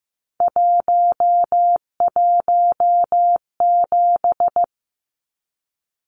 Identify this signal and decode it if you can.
Morse code, heard as a keyed tone, about 15 wpm, 700 Hz. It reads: 117
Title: Numbers - 3-Digit 15wpm